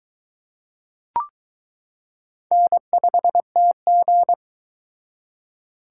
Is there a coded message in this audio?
Yes: Morse code from 2.5 to 4.3 s, heard as a keyed tone, about 22 wpm, 700 Hz. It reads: N5TG